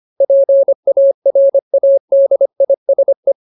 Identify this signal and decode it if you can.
Morse code, heard as a keyed tone, about 25 wpm, 550 Hz. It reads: PARADISE
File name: Ultimate Word List / 25WPM Ultimate